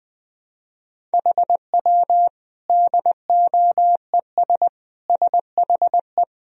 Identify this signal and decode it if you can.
Morse code, heard as a keyed tone, about 20 wpm, 700 Hz. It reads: HW DOES SHE
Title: Sentences from Top 200 Words and Abbreviations 20wpm